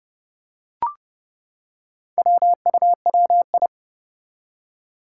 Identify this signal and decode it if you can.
Morse code, heard as a keyed tone, about 30 wpm, 700 Hz. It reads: WUWI
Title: Four Challenging Letter-Number Combinations 30wpm